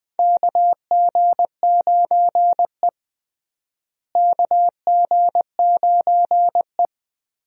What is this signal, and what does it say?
Morse code, heard as a keyed tone, about 20 wpm, 700 Hz. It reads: KG9E KG9E
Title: USA Call Signs - 4 Characters - Repeated 2x - 20wpm